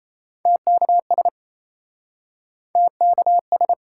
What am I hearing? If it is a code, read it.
Morse code, heard as a keyed tone, about 36 wpm, 700 Hz. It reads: TKS TKS